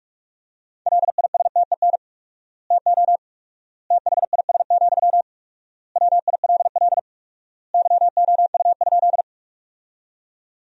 Morse code, heard as a keyed tone, about 42 wpm, 700 Hz. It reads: LISTEN TO THIS, WILL YOU?
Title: 4x Word-Spacing - Sentences from Top 400 Words - 45wpm